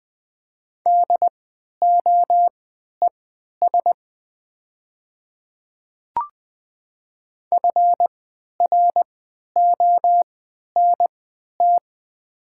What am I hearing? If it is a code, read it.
Morse code, heard as a keyed tone, about 20 wpm, spaced as for 11 wpm, 700 Hz. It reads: DOES FRONT
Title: 3x Character Spacing - Top 500 Words - 20wpm